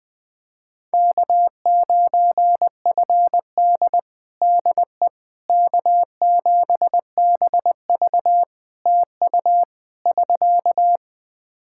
Morse code, heard as a keyed tone, about 20 wpm, 700 Hz. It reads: K9FD DE K7BV TU <SK>
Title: Practice QSO - Exchange 4 - Ending - 20wpm